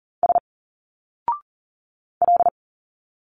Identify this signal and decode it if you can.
Morse code, heard as a keyed tone, about 39 wpm, 700 Hz. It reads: S L
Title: Single Letters 40wpm